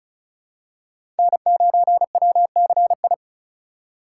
Morse code, heard as a keyed tone, about 36 wpm, 700 Hz. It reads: N9WCI
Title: USA Call Signs - 5 Characters - 3x Character Spacing - 35wpm